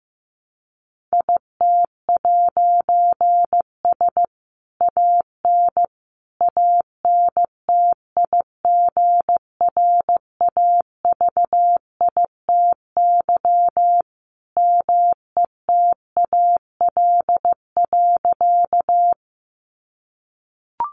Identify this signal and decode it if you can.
Morse code, heard as a keyed tone, about 15 wpm, 700 Hz. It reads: IT'S AN ANTIGRAVITY METAL.
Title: A Sound Of Thunder 15wpm